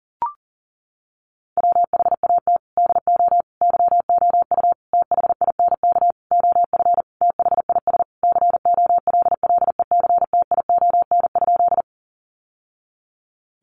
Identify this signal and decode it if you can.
Morse code, heard as a keyed tone, about 40 wpm, 700 Hz. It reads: WHAT DO YOU THINK OF THIS COLLECTION?
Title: Sentences from Top 1200 Words 40wpm